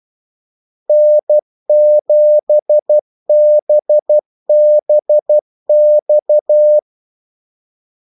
Morse code, heard as a keyed tone, about 12 wpm, 600 Hz. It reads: N7BBX
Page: 2 ekstra ordmellemrum mellem kaldesignalerne